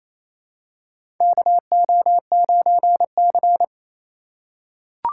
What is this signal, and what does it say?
Morse code, heard as a keyed tone, about 28 wpm, 700 Hz. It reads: KO9C
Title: USA Call Signs - 4 Characters - 3x Character Spacing - 28wpm